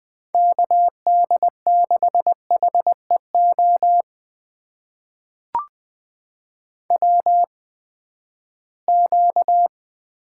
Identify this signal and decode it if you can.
Morse code, heard as a keyed tone, about 20 wpm, spaced as for 14 wpm, 700 Hz. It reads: KD6HEO WQ
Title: USA Call Signs - 8x Character Spacing - 20wpm